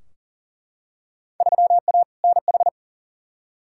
Morse code, heard as a keyed tone, about 40 wpm, 700 Hz. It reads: 3A NH
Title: Field Day Exchanges - 40wpm